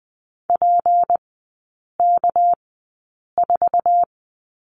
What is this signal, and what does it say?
Morse code, heard as a keyed tone, about 20 wpm, spaced as for 8 wpm, 700 Hz. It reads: PK4
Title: 198 Lesson 32e - Letter k - Three Characters 20wpm